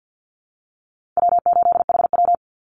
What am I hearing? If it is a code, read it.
Morse code, heard as a keyed tone, about 48 wpm, 700 Hz. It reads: W8HW
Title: Easy Call Signs Repeated 2x - 50wpm